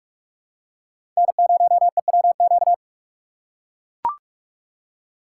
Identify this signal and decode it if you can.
Morse code, heard as a keyed tone, about 41 wpm, 700 Hz. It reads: N0EWQ
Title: USA Call Signs - 2x Character Spacing - 45wpm